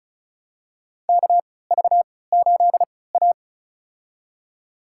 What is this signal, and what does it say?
Morse code, heard as a keyed tone, about 34 wpm, spaced as for 20 wpm, 700 Hz. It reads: KV8A
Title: USA Call Signs - 4 Characters - 3x Character Spacing - 35wpm